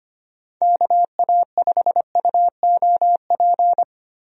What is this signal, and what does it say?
Morse code, heard as a keyed tone, about 25 wpm, 700 Hz. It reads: KA5UOP
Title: USA Call Signs - 6 Characters - 25wpm